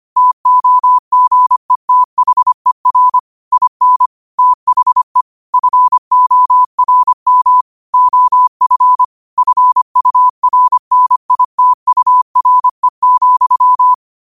Morse code, heard as a keyed tone, about 25 wpm, 1000 Hz. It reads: TOGETHER IN THE FORM OF FURNITURE,
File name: qotdfiles / 25wpm